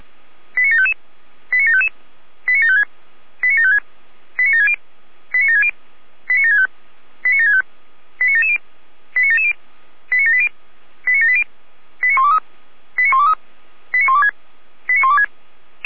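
Five-tone selective call (ZVEI), heard as quick runs of five tones, tones of about 70 ms. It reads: 89755 89755 89756 89756 89760 89760 89765 89765 89770 89770 89780 89780 89123 89123 89126 89126
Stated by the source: ZVEI five-tone selective call received by SDR.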